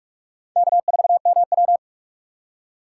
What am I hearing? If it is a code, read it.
Morse code, heard as a keyed tone, about 46 wpm, 700 Hz. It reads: K4MW
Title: USA Call Signs - 2x Character Spacing - 45wpm